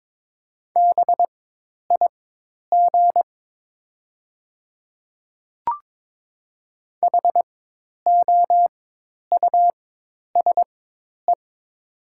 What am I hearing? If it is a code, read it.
Morse code, heard as a keyed tone, about 22 wpm, spaced as for 10 wpm, 700 Hz. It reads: BIG HOUSE
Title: 4x Character Spacing - Top 300 Words - 22wpm